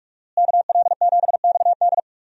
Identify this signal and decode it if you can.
Morse code, heard as a keyed tone, about 46 wpm, 700 Hz. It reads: KL7XD
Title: DXCC-100 Call Signs - 5 Characters - Repeated 3x - 45wpm